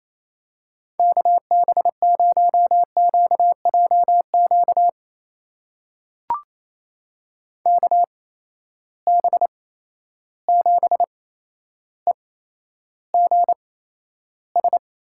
Morse code, heard as a keyed tone, about 28 wpm, spaced as for 12 wpm, 700 Hz. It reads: KB0QJQ KB7EGS
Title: USA Call Signs - 6 Characters - 8x Character Spacing - 28wpm